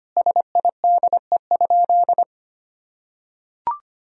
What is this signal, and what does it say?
Morse code, heard as a keyed tone, about 25 wpm, 700 Hz. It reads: SIDE?
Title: Sentences from Top 1000 Words - Review Part 1 - 25wpm